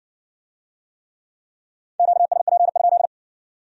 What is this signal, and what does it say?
Morse code, heard as a keyed tone, about 56 wpm, 700 Hz. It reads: QSP?